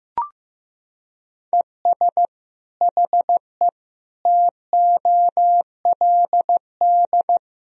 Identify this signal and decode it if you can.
Morse code, heard as a keyed tone, about 14 wpm, 700 Hz. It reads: ES HE TOLD
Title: Sentences from Top 500 Words and Abbreviations 15wpm